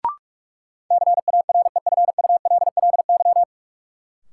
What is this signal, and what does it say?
Morse code, heard as a keyed tone, about 46 wpm, 700 Hz. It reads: CAREFULLY
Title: Sentences from Top 1300 Words - New Word Drill Alt - 45wpm